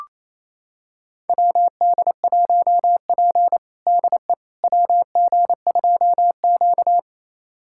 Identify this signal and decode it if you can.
Morse code, heard as a keyed tone, about 28 wpm, 700 Hz. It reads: WD1P DE WG2Q
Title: USA Callsign DE Callsign - 4 Characters - 28wpm